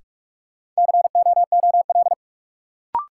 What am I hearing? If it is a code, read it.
Morse code, heard as a keyed tone, about 44 wpm, 700 Hz. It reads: COOL
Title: Sentences from Top 1200 Words - New Word Drill - 45wpm